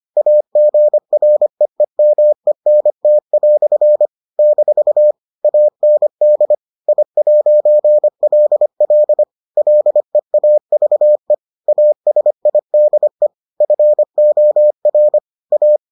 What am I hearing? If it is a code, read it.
Morse code, heard as a keyed tone, about 25 wpm, 600 Hz. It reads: AGREEMENT" - AND I'LL LEAVE ASIDE FOR A